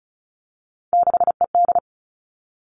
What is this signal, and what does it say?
Morse code, heard as a keyed tone, about 36 wpm, 700 Hz. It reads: 6ED
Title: Three Challenging Letter-Number Combinations 35wpm